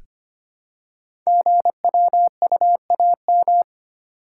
Morse code, heard as a keyed tone, about 25 wpm, 700 Hz. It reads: GWUAM
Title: Code Groups - 5 Letters - 25wpm